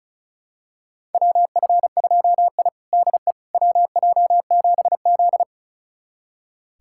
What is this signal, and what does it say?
Morse code, heard as a keyed tone, about 36 wpm, 700 Hz. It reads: WF2I DE WJ7Z
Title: USA Callsign DE Callsign - 4 Characters - 35wpm